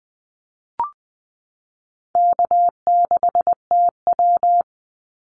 Morse code, heard as a keyed tone, about 20 wpm, 700 Hz. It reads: K6TW
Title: NA SOTA Activator Callsigns 20wpm